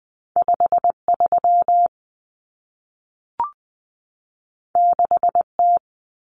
Morse code, heard as a keyed tone, about 20 wpm, 700 Hz. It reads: 53 6T
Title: Common Cut Numbers - 2-Digit 20wpm